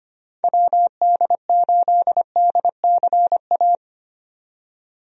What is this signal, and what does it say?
Morse code, heard as a keyed tone, about 25 wpm, 700 Hz. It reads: WD8DCA
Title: USA Call Signs - 6 Characters - 25wpm